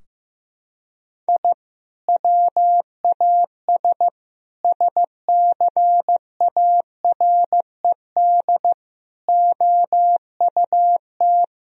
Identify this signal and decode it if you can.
Morse code, heard as a keyed tone, about 15 wpm, 700 Hz. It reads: I WAS SCARED OUT
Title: Sentences from Top 1000 Words - Review Part 4 - 15wpm